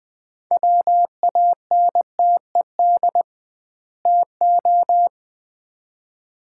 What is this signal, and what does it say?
Morse code, heard as a keyed tone, about 21 wpm, 700 Hz. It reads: WANTED TO
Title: 2x Word-Spacing - Sets of 2 Words - 20wpm